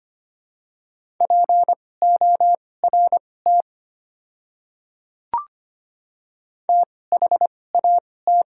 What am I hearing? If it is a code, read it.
Morse code, heard as a keyed tone, about 25 wpm, spaced as for 18 wpm, 700 Hz. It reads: PORT THAT
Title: 2x Character Spacing - Top 200 Words - 25wpm